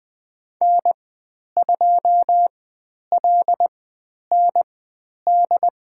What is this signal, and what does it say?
Morse code, heard as a keyed tone, about 20 wpm, spaced as for 10 wpm, 700 Hz. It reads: N2LND
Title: 095 Lesson 16g - Letter c - Call Signs 20wpm